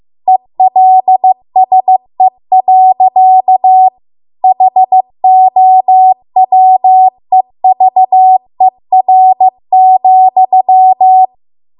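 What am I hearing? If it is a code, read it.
Morse code, heard as a keyed tone, about 15 wpm, 750 Hz. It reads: ELSE. HOWEVER,
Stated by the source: Morse code practice at 15 wpm from the national amateur radio association's code archive.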